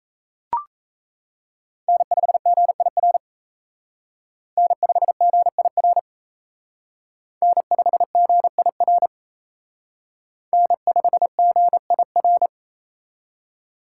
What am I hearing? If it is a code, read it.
Morse code, heard as a keyed tone, about 35 wpm, 700 Hz. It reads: N5GIR N5GIR N5GIR N5GIR
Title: Speed Racing - USA 1x3 Call Signs 28wpm